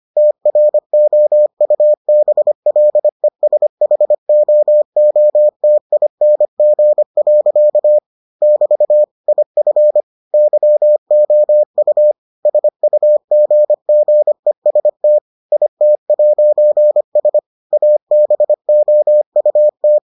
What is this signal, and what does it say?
Morse code, heard as a keyed tone, about 25 wpm, 600 Hz. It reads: TROUBLESHOOTING. <BT> IF YOU SUGGEST IT'S ABOUT